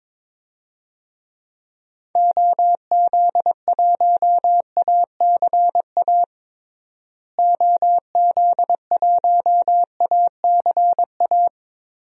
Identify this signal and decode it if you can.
Morse code, heard as a keyed tone, about 22 wpm, 700 Hz. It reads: OZ1ACA OZ1ACA
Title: DXCC-100 Call Signs - 6 Characters - Repeated 3x - 22wpm